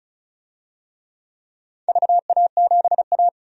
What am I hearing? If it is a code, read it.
Morse code, heard as a keyed tone, about 36 wpm, 700 Hz. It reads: VA7A
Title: DXCC-100 Call Signs - 4 Characters - 35wpm